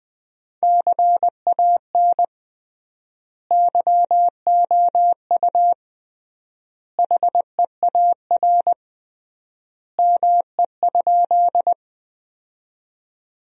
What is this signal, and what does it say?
Morse code, heard as a keyed tone, about 20 wpm, 700 Hz. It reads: CAN YOU HEAR ME?
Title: 3x Word-Spacing - Sentences from Top 300 Words - 20wpm